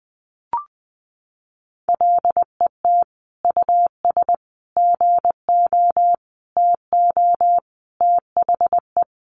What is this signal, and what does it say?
Morse code, heard as a keyed tone, about 20 wpm, 700 Hz. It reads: LET US GO TO THE
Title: Sentences from Top 1400 Words 20wpm